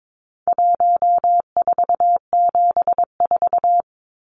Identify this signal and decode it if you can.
Morse code, heard as a keyed tone, about 22 wpm, 700 Hz. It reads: 1474
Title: Numbers - 4-Digit 22wpm